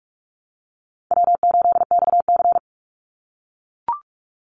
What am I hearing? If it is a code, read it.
Morse code, heard as a keyed tone, about 44 wpm, 700 Hz. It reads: W8XC
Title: Easy Call Signs - 45wpm